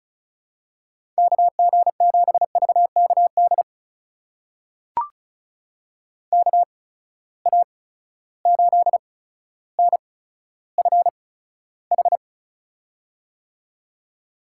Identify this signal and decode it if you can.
Morse code, heard as a keyed tone, about 36 wpm, spaced as for 15 wpm, 700 Hz. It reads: KG7VKD KA8NFH
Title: USA Call Signs - 6 Characters - 8x Character Spacing - 35wpm